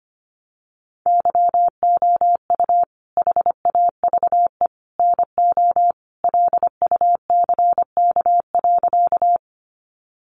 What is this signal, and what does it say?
Morse code, heard as a keyed tone, about 25 wpm, 700 Hz. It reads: YOU HAVE NO LUCK.
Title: Sentences from Top 1500 Words 25wpm